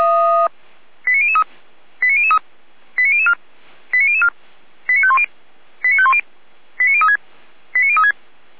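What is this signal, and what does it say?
Five-tone selective call (ZVEI), heard as quick runs of five tones, tones of about 70 ms. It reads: 89003 89003 89004 89004 89510 89510 89036 89036
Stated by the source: ZVEI five-tone selective call received with an SDR, logged on the Feuerwehr Orlamünde audio server.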